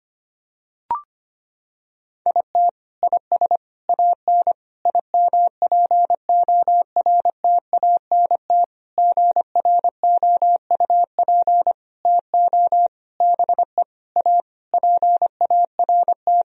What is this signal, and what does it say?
Morse code, heard as a keyed tone, about 25 wpm, 700 Hz. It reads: IT IS AN IMPORTANT GROUP TO BE A PART
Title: Sentences from Top 1000 Words - Review Part 1 - 25wpm